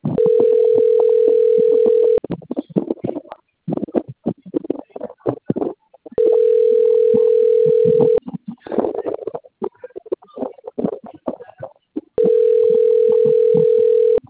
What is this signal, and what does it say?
Ringback tone (North America), 440 + 480 Hz.